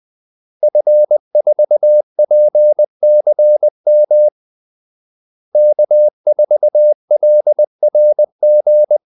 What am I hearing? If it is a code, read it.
Morse code, heard as a keyed tone, about 20 wpm, 600 Hz. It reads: F4PCM K4LRG
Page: Kaldesignaler 20 wpm